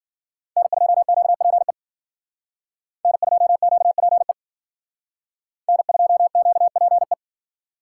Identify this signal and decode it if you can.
Morse code, heard as a keyed tone, about 49 wpm, 700 Hz. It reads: N2QPE N2QPE N2QPE
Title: Speed Racing - USA Call Signs 40wpm